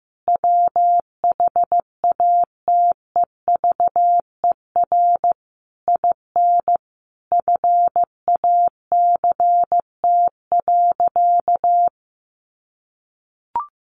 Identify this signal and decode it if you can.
Morse code, heard as a keyed tone, about 15 wpm, 700 Hz. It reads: WHATEVER IN FACT.
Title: Sentences from Top 1200 Words 15wpm